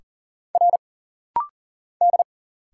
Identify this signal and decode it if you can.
Morse code, heard as a keyed tone, about 40 wpm, 700 Hz. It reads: R D
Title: Single Letters - Rapid-Fire 40wpm